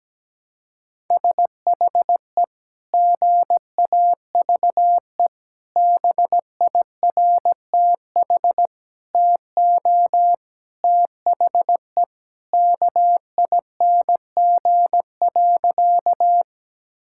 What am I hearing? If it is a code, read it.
Morse code, heard as a keyed tone, about 17 wpm, 700 Hz. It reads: SHE GAVE BIRTH TO THE KING.
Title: Sentences from Top 1000 Words - Review Part 5 - 17wpm